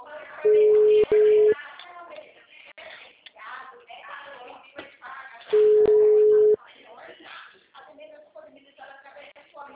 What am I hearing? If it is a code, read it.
Ringback tone (Europe, 425 Hz), with speech.
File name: out- -2010- - - .43025